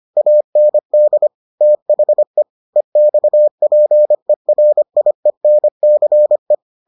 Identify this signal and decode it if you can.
Morse code, heard as a keyed tone, about 25 wpm, 600 Hz. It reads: AND THE EXPERIENCE